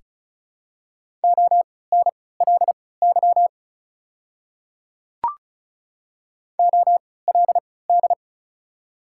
Morse code, heard as a keyed tone, about 34 wpm, spaced as for 20 wpm, 700 Hz. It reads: ONLY OLD
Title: 3x Character Spacing - Top 200 Words - 35wpm